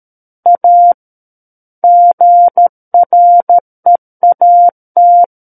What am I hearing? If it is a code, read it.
Morse code, heard as a keyed tone, about 13 wpm, 700 Hz. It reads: A GREAT
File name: Book One / 04-Chapter 4 - 13 WPM